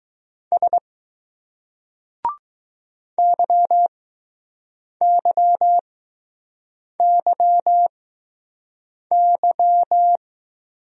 Morse code, heard as a keyed tone, about 20 wpm, spaced as for 6 wpm, 700 Hz. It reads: S YYYY